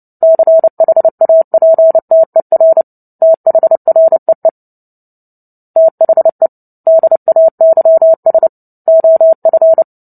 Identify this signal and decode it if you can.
Morse code, heard as a keyed tone, about 30 wpm, 650 Hz. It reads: CHAPTER THREE THE DAYS OF
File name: Book Two / 03-Chapter 3 - 29 WPM